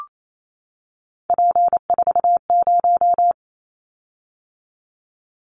Morse code, heard as a keyed tone, about 28 wpm, 700 Hz. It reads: P40
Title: Callsign Prefixes for Top 100 DXCC - 28wpm